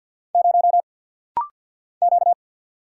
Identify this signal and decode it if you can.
Morse code, heard as a keyed tone, about 50 wpm, 700 Hz. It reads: 0 Q
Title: Single Letter-Number - Rapid-Fire 50wpm